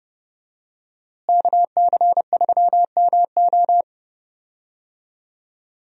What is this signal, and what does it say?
Morse code, heard as a keyed tone, about 30 wpm, 700 Hz. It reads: KC3MO
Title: USA Call Signs - 5 Characters - Repeated 3x - 30wpm